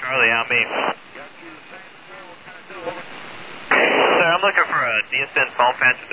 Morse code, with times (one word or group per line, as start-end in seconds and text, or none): none